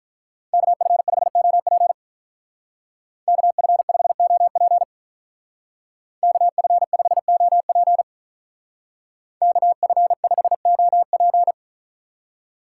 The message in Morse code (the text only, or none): KF5OP KF5OP KF5OP KF5OP